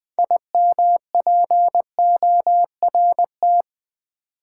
Morse code, heard as a keyed tone, about 20 wpm, 700 Hz. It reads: IMPORT
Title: Build A Word - 3 Letters At A Time - 9-Letter Words - 20wpm